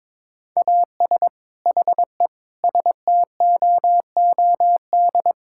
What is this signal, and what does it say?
Morse code, heard as a keyed tone, about 22 wpm, 700 Hz. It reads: AS HE STOOD
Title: Sentences from Top 1600 Words 22wpm